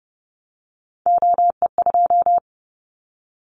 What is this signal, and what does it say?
Morse code, heard as a keyed tone, about 29 wpm, 700 Hz. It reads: OE2
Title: Callsign Prefixes for Top 25 DXCC - Repeated 3x - 30wpm